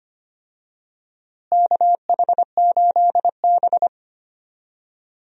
Morse code, heard as a keyed tone, about 25 wpm, 700 Hz. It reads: KH8B
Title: USA Call Signs - 4 Characters - Repeated 2x - 25wpm